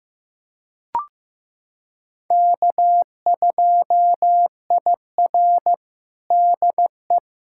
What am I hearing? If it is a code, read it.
Morse code, heard as a keyed tone, about 15 wpm, 700 Hz. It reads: K2IR DE
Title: USA Callsign DE Callsign - 4 Characters - 15wpm